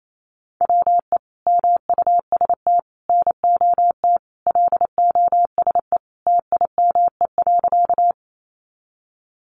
Morse code, heard as a keyed tone, about 28 wpm, 700 Hz. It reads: WE MUST NOT LOSE TIME.
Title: Sentences from Top 1100 Words 28wpm